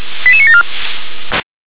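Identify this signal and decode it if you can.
Five-tone selective call (ZVEI), heard as quick runs of five tones, tones of about 70 ms. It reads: 88974
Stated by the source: ZVEI five-tone selective call, code 88994, received with an SDR.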